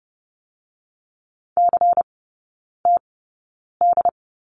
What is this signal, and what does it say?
Morse code, heard as a keyed tone, about 30 wpm, spaced as for 9 wpm, 690 Hz. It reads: CTD
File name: Lesson 17e - Letter u - Three Characters 30wpm